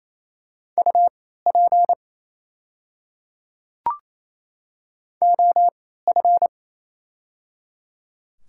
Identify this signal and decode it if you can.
Morse code, heard as a keyed tone, about 28 wpm, spaced as for 16 wpm, 700 Hz. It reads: UP OF
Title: 3x Character Spacing - Top 100 Words - 28wpm